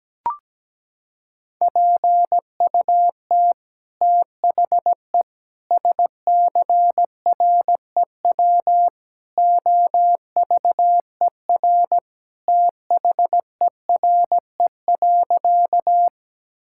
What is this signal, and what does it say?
Morse code, heard as a keyed tone, about 17 wpm, 700 Hz. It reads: PUT THE SCREW OVER THERE.
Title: Sentences from Top 1600 Words 17wpm